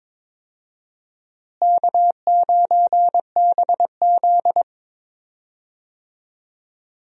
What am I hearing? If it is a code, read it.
Morse code, heard as a keyed tone, about 22 wpm, 700 Hz. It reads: K9BZ
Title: USA Call Signs - 4 Characters - Repeated 3x - 22wpm